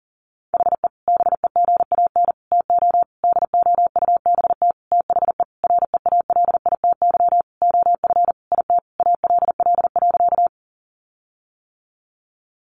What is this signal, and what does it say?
Morse code, heard as a keyed tone, about 40 wpm, 700 Hz. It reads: HE BEGAN TO DOUBT THE REALITY OF IT ALL.
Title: Sentences from Top 1500 Words 40wpm